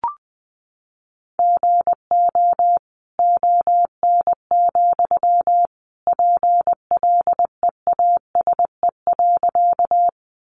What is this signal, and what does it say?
Morse code, heard as a keyed tone, about 20 wpm, 700 Hz. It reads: GO ON, PLEASE.
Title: Sentences from Top 1500 Words 20wpm